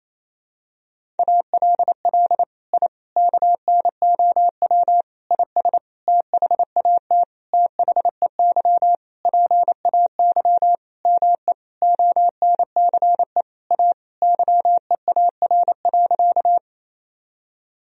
Morse code, heard as a keyed tone, about 28 wpm, 700 Hz. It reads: ALL I KNOW IS THAT THEY PAY ME ONCE A YEAR.
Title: Sentences from Top 1000 Words - Review Part 1 - 28wpm